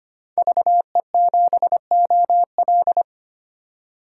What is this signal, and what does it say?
Morse code, heard as a keyed tone, about 25 wpm, 700 Hz. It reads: VE7OL
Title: DXCC-100 Call Signs - 5 Characters - 25wpm